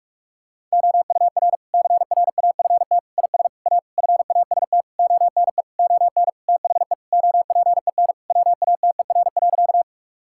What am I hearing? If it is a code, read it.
Morse code, heard as a keyed tone, about 41 wpm, 700 Hz. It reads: OUR CRAFT IS A FAST ONE ON THE OPEN WATER.
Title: Sentences from Top 1000 Words - Review Part 6 - 45wpm